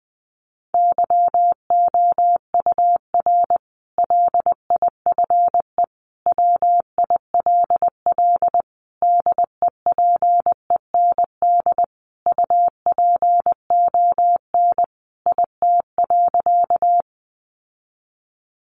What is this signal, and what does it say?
Morse code, heard as a keyed tone, about 20 wpm, 700 Hz. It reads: YOUR LIFE WILL DEPEND UPON IT.
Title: Sentences from Top 1300 Words 20wpm